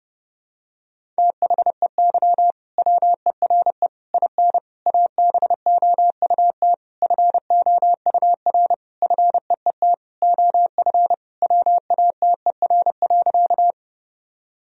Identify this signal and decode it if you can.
Morse code, heard as a keyed tone, about 32 wpm, 700 Hz. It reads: THEY WERE IN ABOUT FOUR FEET OF WATER.
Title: Sentences from Top 1000 Words - Review Part 1 - 30wpm